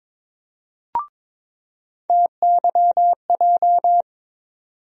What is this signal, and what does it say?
Morse code, heard as a keyed tone, about 22 wpm, 700 Hz. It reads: TYJ
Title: USA Call Sign Postfixes - 3 Letters 22wpm